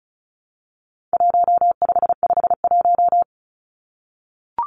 Morse code, heard as a keyed tone, about 34 wpm, 700 Hz. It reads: 1551
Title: Numbers - Contest Style - 4-Digit 35wpm